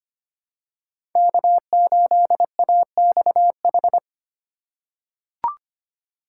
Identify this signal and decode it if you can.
Morse code, heard as a keyed tone, about 25 wpm, 700 Hz. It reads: K8AXH
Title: USA Call Signs - 5 Characters - 3x Character Spacing - 25wpm